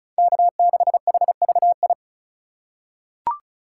Morse code, heard as a keyed tone, about 36 wpm, 700 Hz. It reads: K6HVI
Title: USA Call Signs - 5 Characters - 2x Character Spacing - 35wpm